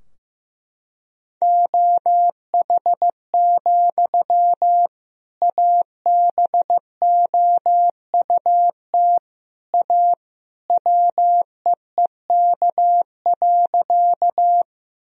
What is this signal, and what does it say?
Morse code, heard as a keyed tone, about 15 wpm, 700 Hz. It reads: OH, ABOUT A WEEK.
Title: Sentences from Top 1000 Words - Review Part 2 - 15wpm